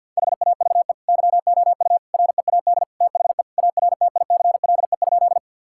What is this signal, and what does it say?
Morse code, heard as a keyed tone, about 50 wpm, 700 Hz. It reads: HAVE YOU READ THE ARTICLE?
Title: Sentences from Top 1000 Words - Review Part 5 - 50wpm